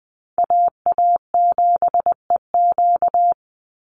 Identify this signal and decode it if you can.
Morse code, heard as a keyed tone, about 20 wpm, 700 Hz. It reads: AA7EQ
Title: NA SOTA Chaser Callsigns 20wpm